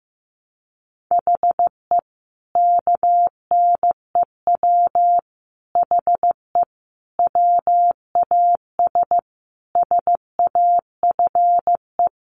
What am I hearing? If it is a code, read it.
Morse code, heard as a keyed tone, about 15 wpm, 700 Hz. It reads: HE KNEW HE WAS SAFE
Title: Sentences from Top 900 Words 15wpm